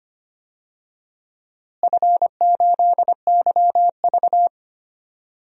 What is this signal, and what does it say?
Morse code, heard as a keyed tone, about 25 wpm, 700 Hz. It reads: F8YV
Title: DXCC-100 Call Signs - 4 Characters - Repeated 3x - 25wpm